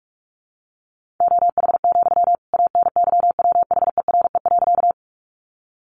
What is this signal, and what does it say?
Morse code, heard as a keyed tone, about 44 wpm, 700 Hz. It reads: OH, ANYWHERE.
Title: Sentences from Top 1400 Words 45wpm